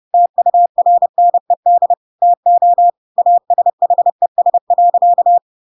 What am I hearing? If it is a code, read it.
Morse code, heard as a keyed tone, about 30 wpm, 700 Hz. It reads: TURNED TO ASHES.
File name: Pinocchio / Pinocchio ch6 30 30 ik1hge